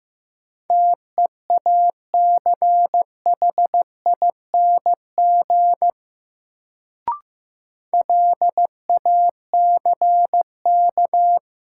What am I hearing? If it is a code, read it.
Morse code, heard as a keyed tone, about 15 wpm, 700 Hz. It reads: TEACHING LACK
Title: Sentences from Top 1100 Words - New Word Drill Alt - 15wpm